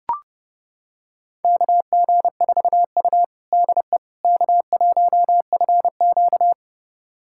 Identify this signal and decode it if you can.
Morse code, heard as a keyed tone, about 30 wpm, 700 Hz. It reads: KG4U DE K1FQ
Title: USA Callsign DE Callsign - 4 Characters - 30wpm